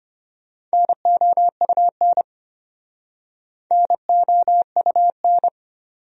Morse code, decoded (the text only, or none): NOUN NOUN